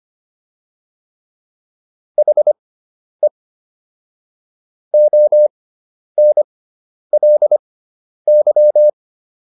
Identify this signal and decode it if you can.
Morse code, heard as a keyed tone, about 25 wpm, spaced as for 10 wpm, 600 Hz. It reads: HE ONLY